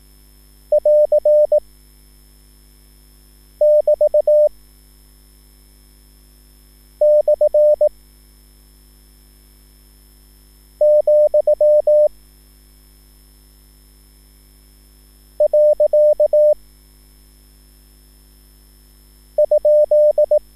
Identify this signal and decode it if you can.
Morse code, heard as a keyed tone, about 18 wpm, 600 Hz. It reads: <AR> <BT> / , . ?
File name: QSL / CW lesson